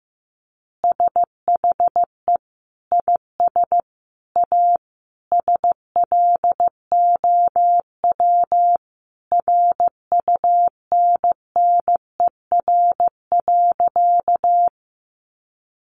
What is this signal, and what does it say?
Morse code, heard as a keyed tone, about 14 wpm, 700 Hz. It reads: SHE IS A SLOW RUNNER.
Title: Additional Easy Phrases 15wpm